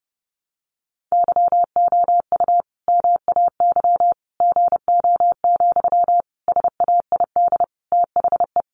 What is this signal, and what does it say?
Morse code, heard as a keyed tone, about 30 wpm, 700 Hz. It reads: YOU MAY GO, SAID THE
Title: Sentences from Top 400 Words 30wpm